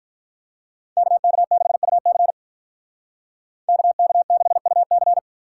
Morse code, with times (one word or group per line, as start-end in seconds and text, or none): 1.0-2.3 KK6UC
3.7-5.2 KK6UC